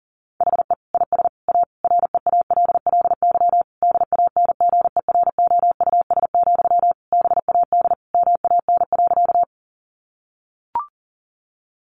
Morse code, heard as a keyed tone, about 40 wpm, 700 Hz. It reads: HE IS A REALLY DANGEROUS, BAD MAN.
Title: Sentences from Top 1200 Words 40wpm